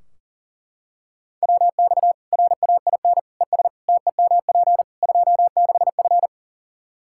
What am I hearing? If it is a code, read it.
Morse code, heard as a keyed tone, about 37 wpm, 700 Hz. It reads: WX RAIN ES TEMP 26F
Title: Practice QSO - Exchange 2 - Info - 40wpm